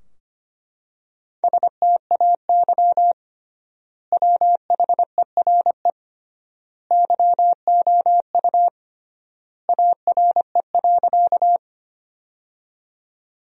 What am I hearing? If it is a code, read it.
Morse code, heard as a keyed tone, about 25 wpm, 700 Hz. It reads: STAY WHERE YOU ARE.
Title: 3x Word-Spacing - Sentences from Top 500 Words - 25wpm